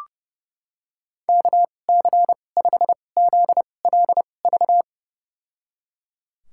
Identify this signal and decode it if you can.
Morse code, heard as a keyed tone, about 32 wpm, spaced as for 22 wpm, 700 Hz. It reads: KC5ZLV
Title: USA Call Signs - 6 Characters - 2x Character Spacing - 30wpm